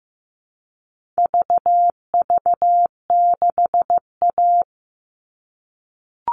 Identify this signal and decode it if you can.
Morse code, heard as a keyed tone, about 15 wpm, 700 Hz. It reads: VV6A
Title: All Cut Numbers - 4-Digit 15wpm